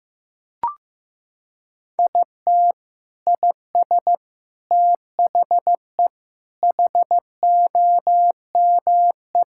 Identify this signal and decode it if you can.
Morse code, heard as a keyed tone, about 15 wpm, 700 Hz. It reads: IT IS THE HOME
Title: Sentences from Top 1000 Words - Review Part 5 - 15wpm